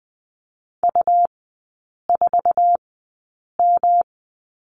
Morse code, heard as a keyed tone, about 20 wpm, spaced as for 8 wpm, 700 Hz. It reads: U4M
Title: 127 Lesson 21e - Number 6 - Three Characters 20wpm